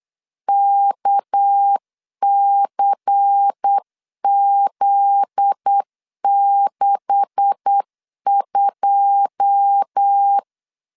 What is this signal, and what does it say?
Morse code, heard as a keyed tone, about 8 wpm, 800 Hz. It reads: KCZ62
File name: LEZIONI / 9 286 8wpm 3425